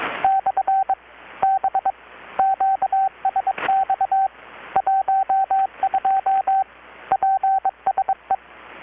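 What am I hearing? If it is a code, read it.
Morse code, heard as a keyed tone, about 21 wpm, 770 Hz, with radio noise. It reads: / B QSX 12 PSE
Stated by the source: Morse from KPH's Night of Nights XX maritime radiotelegraphy event, received in Chiapas, Mexico about 3,800 km away on a Kenwood R-600 with a grounded 10 m long-wire antenna.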